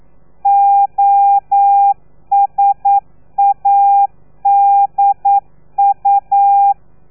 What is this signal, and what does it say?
Morse code, heard as a keyed tone, about 9 wpm, 800 Hz. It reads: OSADU
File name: telegrafia / i2tfj / Lez3 40